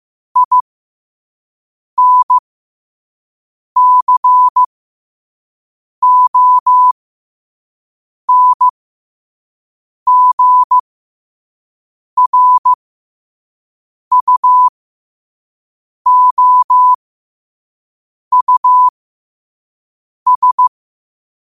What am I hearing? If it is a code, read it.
Morse code, heard as a keyed tone, about 15 wpm, spaced as for 5 wpm, 1000 Hz. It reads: INCONGRUOUS
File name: qotdfiles / 5wpm